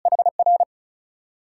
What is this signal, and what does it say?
Morse code, heard as a keyed tone, about 51 wpm, spaced as for 43 wpm, 700 Hz. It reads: HR
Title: Speed-Racing Multi-letter Phonemes - 35wpm